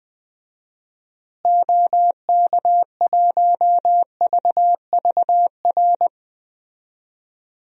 Morse code, heard as a keyed tone, about 20 wpm, 700 Hz. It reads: OK1VVR
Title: DXCC-100 Call Signs - 6 Characters - Repeated 3x - 20wpm